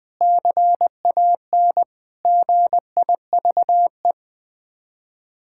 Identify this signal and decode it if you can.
Morse code, heard as a keyed tone, about 20 wpm, 700 Hz. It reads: CAN GIVE
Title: Sets of 8 Words - 20wpm